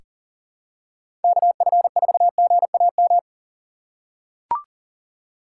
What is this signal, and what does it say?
Morse code, heard as a keyed tone, about 43 wpm, 700 Hz. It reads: KF4GAM
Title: USA Call Signs - 6 Characters - 6x Character Spacing - 40wpm